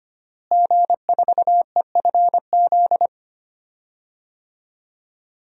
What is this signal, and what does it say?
Morse code, heard as a keyed tone, about 25 wpm, 700 Hz. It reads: G4EFZ
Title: DXCC-100 Call Signs - 5 Characters - Repeated 2x - 25wpm